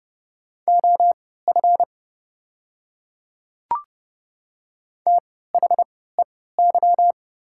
Morse code, heard as a keyed tone, about 30 wpm, spaced as for 17 wpm, 690 Hz. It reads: OF THEY
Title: 3x Character Spacing - Top 100 Words - 30wpm